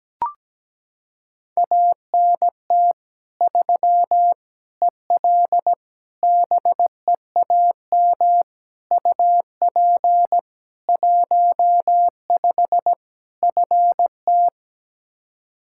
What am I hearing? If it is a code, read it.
Morse code, heard as a keyed tone, about 17 wpm, 700 Hz. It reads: ANT 3 EL BEAM UP 15 FT
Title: Practice QSO - Exchange 2 - Info - 17wpm